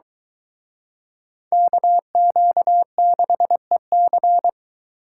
Morse code, heard as a keyed tone, about 23 wpm, 700 Hz. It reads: KQ6EC